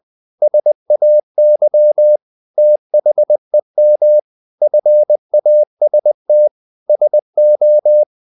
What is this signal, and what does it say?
Morse code, heard as a keyed tone, about 20 wpm, 600 Hz. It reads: SAY THEM FAST SO
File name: the old man and the sea / the old man and the sea part 1 20wpm